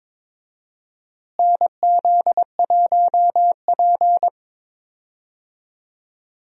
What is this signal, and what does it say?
Morse code, heard as a keyed tone, about 22 wpm, 700 Hz. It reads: NZ1P
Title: USA Call Signs - 4 Characters - Repeated 3x - 22wpm